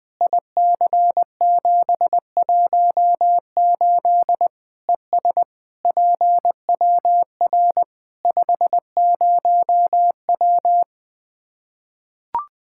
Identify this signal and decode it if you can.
Morse code, heard as a keyed tone, about 20 wpm, 700 Hz. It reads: IC718 ES PWR 50W
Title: Practice QSO - Exchange 2 - Info - 20wpm